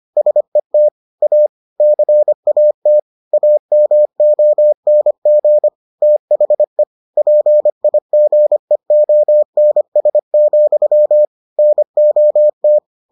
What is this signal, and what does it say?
Morse code, heard as a keyed tone, about 25 wpm, 600 Hz. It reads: SET A CAT AMONG THE PIGEONS, NOT